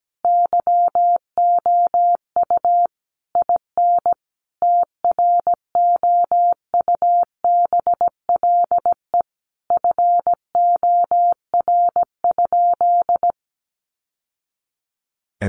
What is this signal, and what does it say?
Morse code, heard as a keyed tone, about 17 wpm, 700 Hz. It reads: YOU IN TROUBLE FOR?
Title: Sentences from Top 1200 Words 17wpm